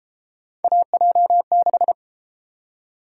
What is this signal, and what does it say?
Morse code, heard as a keyed tone, about 31 wpm, 700 Hz. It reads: AJ6